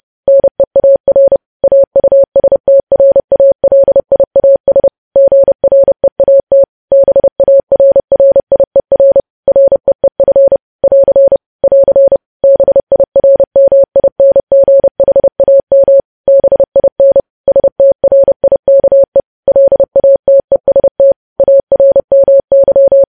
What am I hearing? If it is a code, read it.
Morse code, heard as a keyed tone, about 30 wpm, 550 Hz. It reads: NEAR AUSTRALIAS GREAT BARRIER REEF <AR> <AR> BIRMINGHAM BIN STRIKE LATEST ARMY
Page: news headlines in morse code at 30 wpm